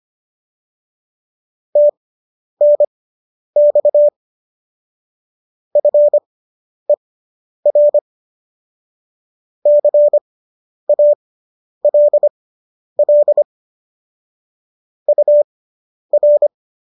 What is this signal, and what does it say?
Morse code, heard as a keyed tone, about 25 wpm, spaced as for 10 wpm, 590 Hz. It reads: TNX FER CALL UR